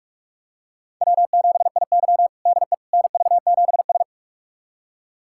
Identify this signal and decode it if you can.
Morse code, heard as a keyed tone, about 46 wpm, 700 Hz. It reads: W7IY DE NV7S
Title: USA Callsign DE Callsign - 4 Characters - 45wpm